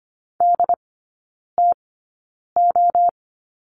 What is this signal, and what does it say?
Morse code, heard as a keyed tone, about 25 wpm, spaced as for 9 wpm, 690 Hz. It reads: DTO
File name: Lesson 14e - Number 2 - Three Characters 25wpm